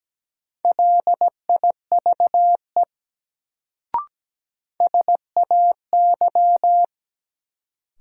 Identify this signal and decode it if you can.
Morse code, heard as a keyed tone, about 17 wpm, 700 Hz. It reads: LIVE SAY
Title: Sentences from Top 200 Words - New Word Drill - 17wpm